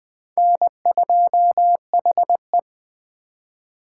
Morse code, heard as a keyed tone, about 20 wpm, 700 Hz. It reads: N2HE